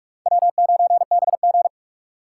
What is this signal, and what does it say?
Morse code, heard as a keyed tone, about 44 wpm, 700 Hz. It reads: W9BG
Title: USA Call Signs - 4 Characters - 3x Character Spacing - 45wpm